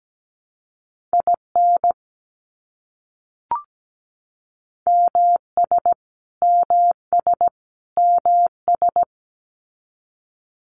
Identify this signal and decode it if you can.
Morse code, heard as a keyed tone, about 17 wpm, 700 Hz. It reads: IN MS MS MS
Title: US States Repeated 3x 17wpm